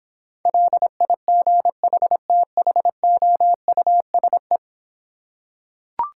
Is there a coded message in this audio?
Yes: Morse code from 0.5 to 4.6 s, heard as a keyed tone, about 26 wpm, 700 Hz. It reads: LIGHTHOUSE